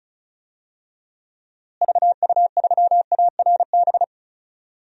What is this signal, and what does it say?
Morse code, heard as a keyed tone, about 36 wpm, 700 Hz. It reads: VU3ARB
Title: DXCC-100 Call Signs - 6 Characters - Repeated 2x - 35wpm